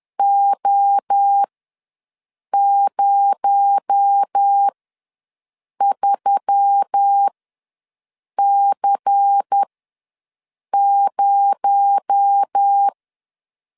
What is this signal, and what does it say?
Morse code, heard as a keyed tone, about 10 wpm, spaced as for 6 wpm, 800 Hz. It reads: O03C0
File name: LEZIONI / 6 PKX0 / 6 PKX0 ws 10wpm 4220